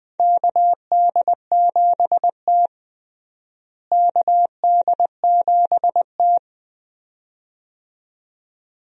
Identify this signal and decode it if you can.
Morse code, heard as a keyed tone, about 20 wpm, 700 Hz. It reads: KD7T KD7T
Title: USA Call Signs - 4 Characters - Repeated 2x - 20wpm